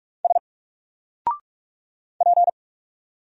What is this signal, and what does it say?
Morse code, heard as a keyed tone, about 46 wpm, 700 Hz. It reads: S P
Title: Single Letter-Number Alt 45wpm